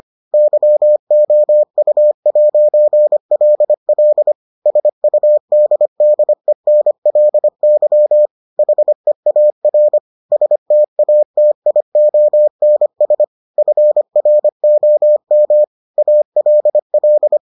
Morse code, heard as a keyed tone, about 25 wpm, 600 Hz. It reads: YOU'LL SUDDENLY HEAR STATIONS FROM ALL